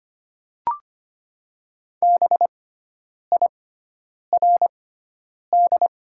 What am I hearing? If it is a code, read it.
Morse code, heard as a keyed tone, about 25 wpm, spaced as for 9 wpm, 700 Hz. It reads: BIRD
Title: 6x Character Spacing - Top 500 Words - 25wpm